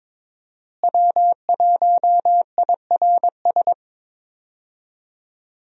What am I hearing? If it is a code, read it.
Morse code, heard as a keyed tone, about 22 wpm, 700 Hz. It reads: W1IRS
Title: USA Call Signs - 5 Characters - Repeated 2x - 22wpm